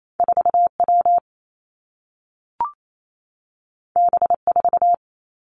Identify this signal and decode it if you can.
Morse code, heard as a keyed tone, about 28 wpm, 700 Hz. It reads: 4 W B 4
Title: Two Challenging Letter-Number Combinations 28wpm